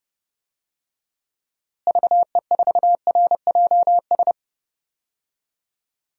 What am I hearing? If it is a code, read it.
Morse code, heard as a keyed tone, about 30 wpm, 700 Hz. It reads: VE4RJS
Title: DXCC-100 Call Signs - 6 Characters - Repeated 3x - 30wpm